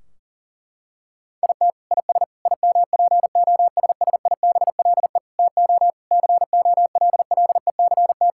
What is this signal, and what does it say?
Morse code, heard as a keyed tone, about 40 wpm, 700 Hz. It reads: IT IS IMPOSSIBLE TO COLLECT
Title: Sentences from Top 1600 Words 40wpm